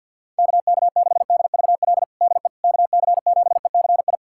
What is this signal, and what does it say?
Morse code, heard as a keyed tone, about 52 wpm, 700 Hz. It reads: KK6DVL DE KC7ECI
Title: USA Callsign DE Callsign - 6 Characters - 50wpm